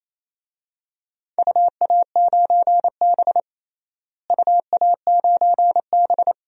Morse code, heard as a keyed tone, about 27 wpm, 700 Hz. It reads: UA9B UA9B
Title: DXCC-100 Call Signs - 4 Characters - Repeated 2x - 28wpm